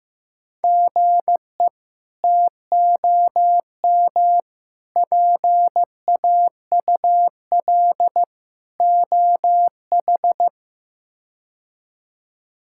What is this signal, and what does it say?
Morse code, heard as a keyed tone, about 15 wpm, 700 Hz. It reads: GE TOM PAUL OH
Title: K1USN SST QSO Practice - 15wpm